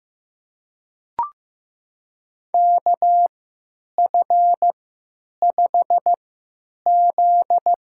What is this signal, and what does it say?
Morse code, heard as a keyed tone, about 15 wpm, spaced as for 9 wpm, 700 Hz. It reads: KF5Z
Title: USA Call Signs - 3x Character Spacing - 15wpm